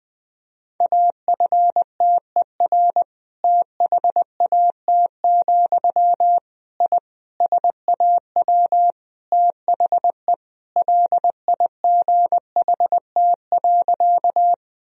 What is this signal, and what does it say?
Morse code, heard as a keyed tone, about 20 wpm, 700 Hz. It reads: AFTER THAT, I SAW THE LIGHT.
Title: Sentences from Top 1000 Words - Review Part 1 - 20wpm